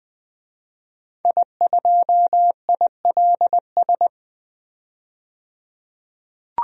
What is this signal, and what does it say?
Morse code, heard as a keyed tone, about 20 wpm, 700 Hz. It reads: I2ILS
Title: DXCC-100 Call Signs - 5 Characters - 20wpm